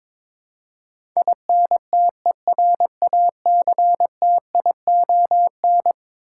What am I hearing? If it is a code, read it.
Morse code, heard as a keyed tone, about 22 wpm, 700 Hz. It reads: INTERACTION
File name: practice / 22wpm / Eleven Letter Words - 22wpm